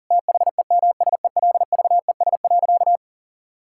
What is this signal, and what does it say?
Morse code, heard as a keyed tone, about 40 wpm, 700 Hz. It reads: THEMSELVES.
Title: Sentences from Top 1000 Words - Review Part 5 - 40wpm